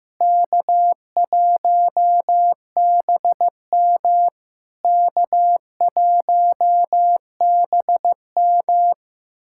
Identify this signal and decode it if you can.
Morse code, heard as a keyed tone, about 14 wpm, 700 Hz. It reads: K1BM K1BM
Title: SOTA QSO Practice 15wpm